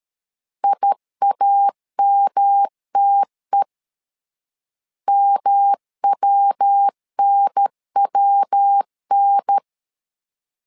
Morse code, heard as a keyed tone, about 13 wpm, 800 Hz. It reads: IAMTE MWNWN
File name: LEZIONI / 3 MWRO / 3 MW 12wpm 1633